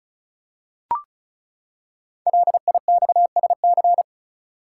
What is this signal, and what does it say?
Morse code, heard as a keyed tone, about 36 wpm, 690 Hz. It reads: LIXSC
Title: Code Groups - 5 Letters - 35wpm